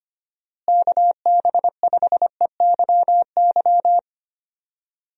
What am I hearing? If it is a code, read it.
Morse code, heard as a keyed tone, about 25 wpm, 700 Hz. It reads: KB5EYY
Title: USA Call Signs - 6 Characters - Repeated 2x - 25wpm